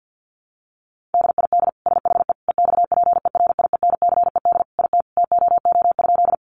Morse code, heard as a keyed tone, about 50 wpm, 700 Hz. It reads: DID SHE EXPERIENCED IT TOO?
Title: Sentences from Top 1100 Words 50wpm